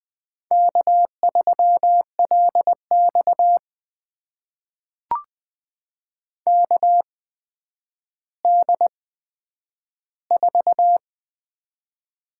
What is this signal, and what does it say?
Morse code, heard as a keyed tone, about 20 wpm, spaced as for 10 wpm, 700 Hz. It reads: K3LX KD4
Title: USA Call Signs - 8x Character Spacing - 20wpm